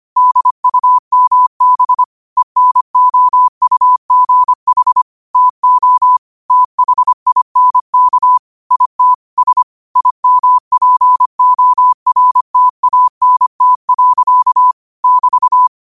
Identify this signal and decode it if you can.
Morse code, heard as a keyed tone, about 25 wpm, 1000 Hz. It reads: DUMB ENOUGH TO THINK IT S IMPORTANT. <BT>
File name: qotdfiles / 25wpm